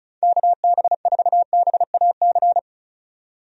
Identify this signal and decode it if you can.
Morse code, heard as a keyed tone, about 36 wpm, 700 Hz. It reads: KB4BAC
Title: USA Call Signs - 6 Characters - 35wpm